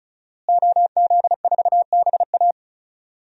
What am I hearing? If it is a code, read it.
Morse code, heard as a keyed tone, about 36 wpm, 700 Hz. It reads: OZ4BA
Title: DXCC-100 Call Signs - 5 Characters - Repeated 3x - 35wpm